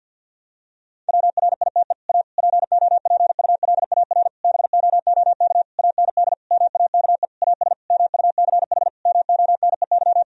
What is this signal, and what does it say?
Morse code, heard as a keyed tone, about 50 wpm, 700 Hz. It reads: WRITE A POPULAR BOOK AND MAKE AS MUCH MONEY
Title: Sentences from Top 1000 Words - Review Part 3 - 50wpm